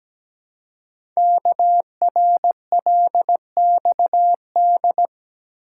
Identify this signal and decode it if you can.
Morse code, heard as a keyed tone, about 17 wpm, 700 Hz. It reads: KRLXD
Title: Code Groups - 5 Letters - 17wpm